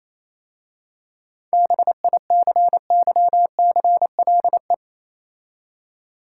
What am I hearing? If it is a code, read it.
Morse code, heard as a keyed tone, about 27 wpm, 690 Hz. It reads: BICYCLE
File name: Easy Words - Speed-Racing with Hints - 22wpm